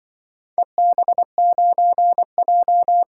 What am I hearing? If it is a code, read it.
Morse code, heard as a keyed tone, about 24 wpm, 700 Hz. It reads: EB9J
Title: Speed-Racing - Four Letter-Number Combinations - 20wpm